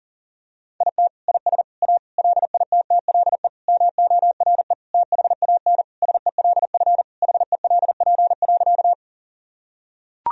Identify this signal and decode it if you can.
Morse code, heard as a keyed tone, about 40 wpm, 700 Hz. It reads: IT IS A LITTLE MORE THAN SELF HELP.
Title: Sentences from Top 1000 Words - Review Part 1 - 40wpm